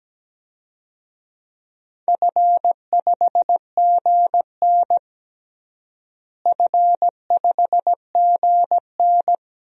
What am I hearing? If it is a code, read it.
Morse code, heard as a keyed tone, about 17 wpm, 700 Hz. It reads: F5GN F5GN
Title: DXCC-100 Call Signs - 4 Characters - Repeated 3x - 17wpm